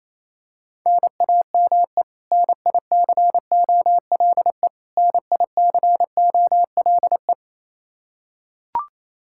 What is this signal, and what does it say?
Morse code, heard as a keyed tone, about 28 wpm, 700 Hz. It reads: NAME NICOLE NICOLE
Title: Practice QSO - Exchange 1 - Beginning - 28wpm